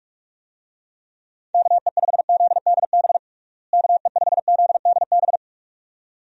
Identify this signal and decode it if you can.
Morse code, heard as a keyed tone, about 46 wpm, 700 Hz. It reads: KE5ZDB KE5ZDB
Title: USA Call Signs - 6 Characters - Repeated 2x - 45wpm